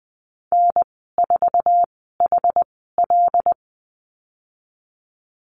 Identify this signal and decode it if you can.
Morse code, heard as a keyed tone, about 19 wpm, spaced as for 14 wpm, 700 Hz. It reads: N4HL
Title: 088 Lesson 15g - Number 5 - Call Signs 20wpm